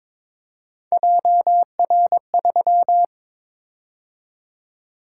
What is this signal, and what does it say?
Morse code, heard as a keyed tone, about 23 wpm, 700 Hz. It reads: JR3
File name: practice / 22wpm / Three Letter-Number Combinations - 22wpm